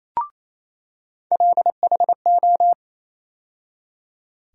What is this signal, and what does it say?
Morse code, heard as a keyed tone, about 27 wpm, 700 Hz. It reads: LHO
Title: USA Call Sign Postfixes - 3 Letters 28wpm